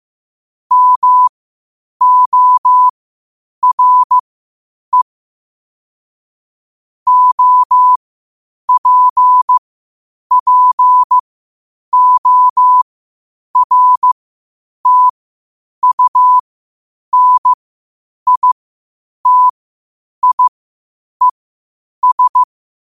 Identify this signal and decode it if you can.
Morse code, heard as a keyed tone, about 15 wpm, spaced as for 8 wpm, 1000 Hz. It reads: MORE OPPORTUNITIES
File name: qotdfiles / 8wpm